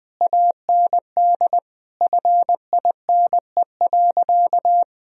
Morse code, heard as a keyed tone, about 20 wpm, 700 Hz. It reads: AND FINE.
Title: Sentences from Top 1000 Words - Review Part 2 - 20wpm